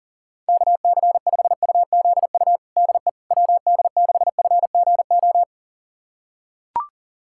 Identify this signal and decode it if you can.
Morse code, heard as a keyed tone, about 40 wpm, 700 Hz. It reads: KC5UZU DE WD6FGO
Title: USA Callsign DE Callsign 40wpm